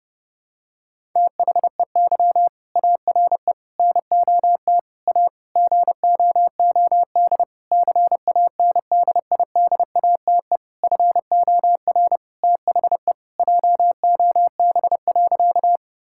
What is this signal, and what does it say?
Morse code, heard as a keyed tone, about 32 wpm, 700 Hz. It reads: THEY ARE NOT A GOOD CANDIDATE FOR THE JOB.
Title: Sentences from Top 1600 Words 30wpm